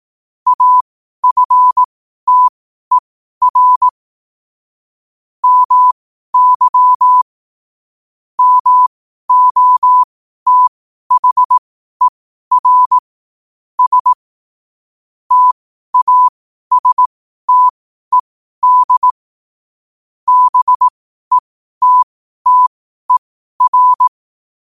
Morse code, heard as a keyed tone, about 18 wpm, spaced as for 12 wpm, 1000 Hz. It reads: AFTER MY MOTHER S TASTED BETTER
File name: qotdfiles / 12wpm